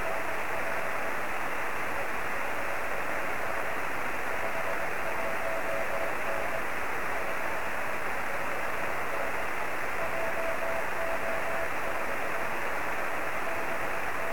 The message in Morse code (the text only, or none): none